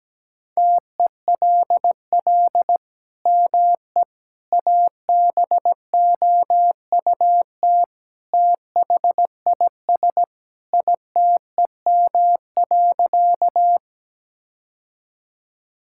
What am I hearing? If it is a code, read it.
Morse code, heard as a keyed tone, about 17 wpm, 700 Hz. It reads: TELL ME ABOUT THIS ITEM.
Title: Sentences from Top 1000 Words - Review Part 6 - 17wpm